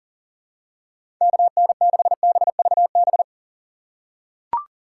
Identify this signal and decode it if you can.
Morse code, heard as a keyed tone, about 40 wpm, 700 Hz. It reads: KN6BVB
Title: USA Call Signs - 6 Characters - 3x Character Spacing - 40wpm